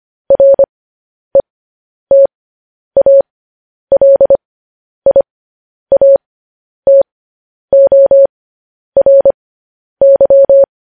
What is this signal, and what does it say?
Morse code, heard as a keyed tone, about 25 wpm, spaced as for 10 wpm, 550 Hz. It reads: RETALIATORY
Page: news headlines in morse code at 10 wpm